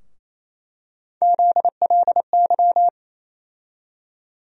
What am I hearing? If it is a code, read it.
Morse code, heard as a keyed tone, about 28 wpm, 700 Hz. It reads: ZLY
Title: USA Call Sign Postfixes - 3 Letters 28wpm